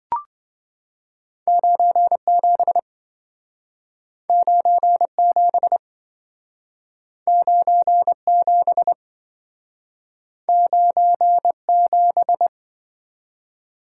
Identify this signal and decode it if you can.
Morse code, heard as a keyed tone, about 26 wpm, 700 Hz. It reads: 97 97 97 97